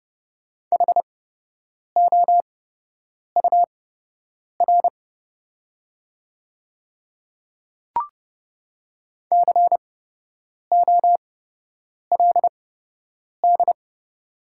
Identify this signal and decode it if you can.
Morse code, heard as a keyed tone, about 30 wpm, spaced as for 8 wpm, 690 Hz. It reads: HOUR COLD
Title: 8x Character Spacing - Top 500 Words - 30wpm